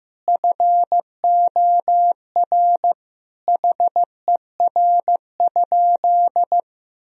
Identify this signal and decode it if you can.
Morse code, heard as a keyed tone, about 15 wpm, 700 Hz. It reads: FOR HER?
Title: Sentences from Top 1000 Words - Review Part 3 - 15wpm